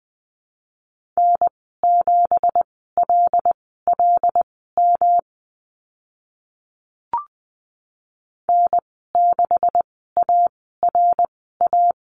Morse code, heard as a keyed tone, about 20 wpm, spaced as for 15 wpm, 700 Hz. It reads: N7LLM N6ARA
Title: 167 Lesson 27g - Number 7 - Call Signs 20wpm